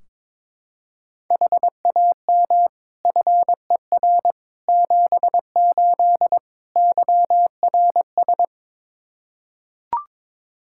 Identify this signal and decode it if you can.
Morse code, heard as a keyed tone, about 21 wpm, 700 Hz. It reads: HAM FER 78 YRS
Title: Practice QSO - Exchange 3 - More Info - 22wpm